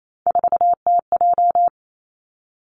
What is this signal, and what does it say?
Morse code, heard as a keyed tone, about 28 wpm, 700 Hz. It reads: 4TJ
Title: Three Challenging Letter-Number Combinations 28wpm